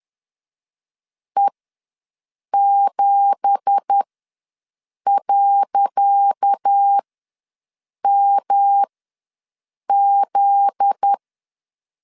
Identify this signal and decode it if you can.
Morse code, heard as a keyed tone, about 10 wpm, spaced as for 6 wpm, 800 Hz. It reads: E7.MZ
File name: LEZIONI / 10 SIGNS / 10C DOT ws 10wpm 4213